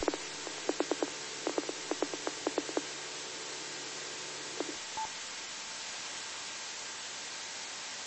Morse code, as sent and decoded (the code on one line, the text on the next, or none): none